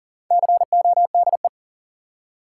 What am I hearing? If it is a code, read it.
Morse code, heard as a keyed tone, about 40 wpm, 700 Hz. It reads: CODE
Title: Four Letter Words - 40wpm